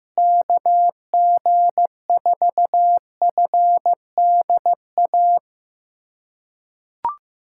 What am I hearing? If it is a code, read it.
Morse code, heard as a keyed tone, about 15 wpm, 700 Hz. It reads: KG4FDA
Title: USA Callsign DE Callsign - 6 Characters - 15wpm